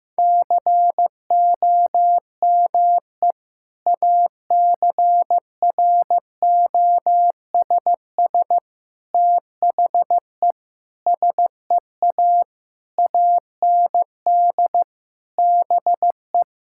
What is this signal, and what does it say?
Morse code, heard as a keyed tone, about 16 wpm, 700 Hz. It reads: COME ACROSS THE SEA AND BE
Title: Sentences from Top 1000 Words - Review Part 5 - 15wpm